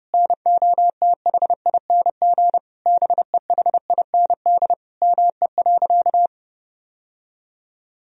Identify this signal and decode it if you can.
Morse code, heard as a keyed tone, about 32 wpm, 700 Hz. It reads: NOTHING BEHIND ME.
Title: Sentences from Top 1000 Words - Review Part 1 - 30wpm